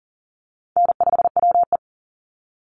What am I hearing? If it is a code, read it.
Morse code, heard as a keyed tone, about 40 wpm, 700 Hz. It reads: N5WE
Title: Call Signs - 40wpm